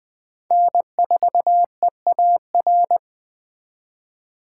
Morse code, heard as a keyed tone, about 20 wpm, 700 Hz. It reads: N4EAR